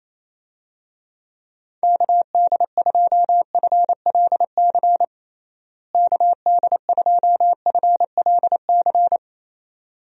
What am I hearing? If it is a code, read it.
Morse code, heard as a keyed tone, about 28 wpm, 700 Hz. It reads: KD2FLC KD2FLC
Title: USA Call Signs - 6 Characters - Repeated 2x - 28wpm